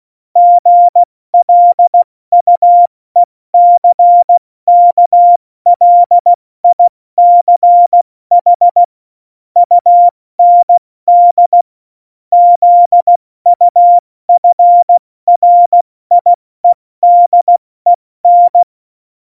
Morse code, heard as a keyed tone, about 16 wpm, 700 Hz. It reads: GLUECKLICH UND ZUFRIEDEN
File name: Morsekurs / Witze / Witz 08. Dezember 2020, 70 F80